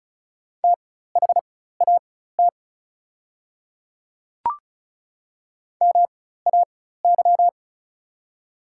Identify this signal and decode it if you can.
Morse code, heard as a keyed tone, about 36 wpm, spaced as for 16 wpm, 700 Hz. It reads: THAT MAY
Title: 4x Character Spacing - Top 100 Words - 35wpm